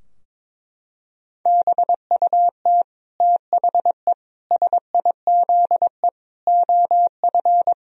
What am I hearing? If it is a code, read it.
Morse code, heard as a keyed tone, about 22 wpm, 700 Hz. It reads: BUT THE SIZE OF
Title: Sentences from Top 1000 Words - Review Part 2 - 22wpm